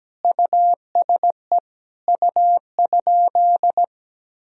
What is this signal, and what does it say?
Morse code, heard as a keyed tone, about 17 wpm, 700 Hz. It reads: USE U?
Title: Sentences from Top 200 Words and Abbreviations 17wpm